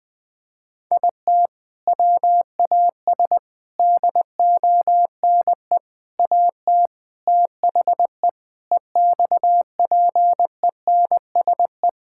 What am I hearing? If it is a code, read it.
Morse code, heard as a keyed tone, about 20 wpm, 700 Hz. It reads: IT WAS DONE AT THE EXPENSE
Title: Sentences with 4-Word Phrases - 20wpm